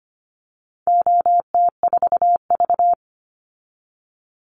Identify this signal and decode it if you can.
Morse code, heard as a keyed tone, about 25 wpm, 700 Hz. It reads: OT4V
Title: Call Signs Repeated 2x - 25wpm